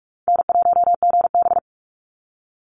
Morse code, heard as a keyed tone, about 44 wpm, 700 Hz. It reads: N1GB
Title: NA SOTA Chaser Callsigns 45wpm